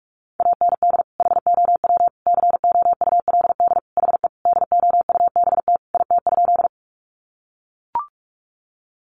Morse code, heard as a keyed tone, about 44 wpm, 700 Hz. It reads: AND HOW COULD HE DOUBT IT?
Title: Sentences from Top 1500 Words 45wpm